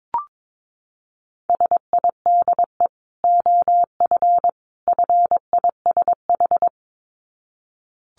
Morse code, heard as a keyed tone, about 22 wpm, 700 Hz. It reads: SIDE OF FISH
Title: Dit Challenge 22wpm